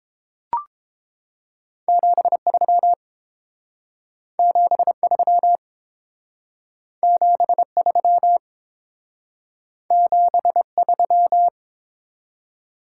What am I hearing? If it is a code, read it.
Morse code, heard as a keyed tone, about 28 wpm, 700 Hz. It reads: 73 73 73 73